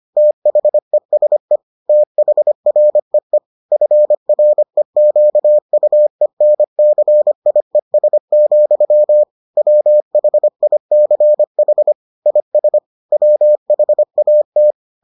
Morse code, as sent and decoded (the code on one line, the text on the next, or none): - .... ... / - .... .-. . / ..-. .-. --.- ..- . -. -.-. .. ... --..-- / .-- .... .. -.-. .... / .. ... / .-- .... .- -
THESE THREE FREQUENCIES, WHICH IS WHAT